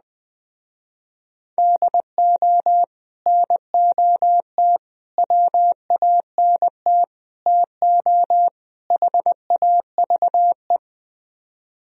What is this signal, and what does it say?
Morse code, heard as a keyed tone, about 20 wpm, 700 Hz. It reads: DO NOT WANT TO HAVE